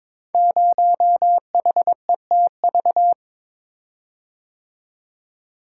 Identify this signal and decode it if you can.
Morse code, heard as a keyed tone, about 23 wpm, 700 Hz. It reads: 0HETV
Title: Code Groups - 5 Letter:Numbers - 22wpm